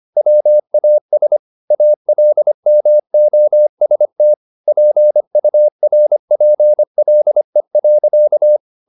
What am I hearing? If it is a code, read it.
Morse code, heard as a keyed tone, about 25 wpm, 600 Hz. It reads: WAS ALMOST PURPLE.
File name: the old man and the sea part 1 25wpm